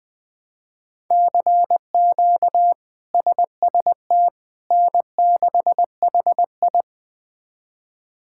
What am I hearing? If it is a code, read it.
Morse code, heard as a keyed tone, about 21 wpm, 700 Hz. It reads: CQ SST N6HI
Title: K1USN SST QSO Practice - 20wpm